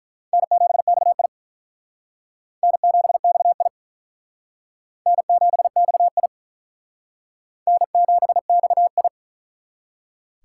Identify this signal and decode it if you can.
Morse code, heard as a keyed tone, about 42 wpm, 700 Hz. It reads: N7XI N7XI N7XI N7XI
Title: Speed Racing - USA 1x2 Call Signs 35wpm